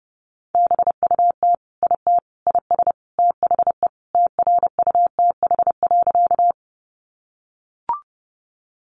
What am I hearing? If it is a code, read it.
Morse code, heard as a keyed tone, about 30 wpm, 700 Hz. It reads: BUT IT IS THE TRUTH.
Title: Sentences from Top 800 Words 30wpm